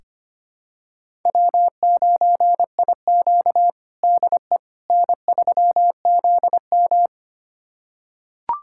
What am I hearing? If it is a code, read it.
Morse code, heard as a keyed tone, about 25 wpm, 700 Hz. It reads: W9IQ DE N3ZM
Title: USA Callsign DE Callsign - 4 Characters - 25wpm